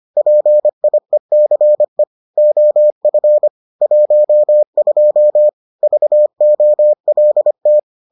Morse code, heard as a keyed tone, about 25 wpm, 600 Hz. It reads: PIECE OF 12 VOLT